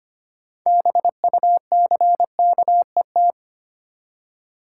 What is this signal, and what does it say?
Morse code, heard as a keyed tone, about 25 wpm, 690 Hz. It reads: BUCKET
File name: practice / 25wpm / Easy Words with Hints - 25wpm